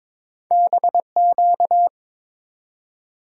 Morse code, heard as a keyed tone, about 21 wpm, 700 Hz. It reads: BQ
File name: practice / 22wpm / Two Letter-Number Combinations - 22wpm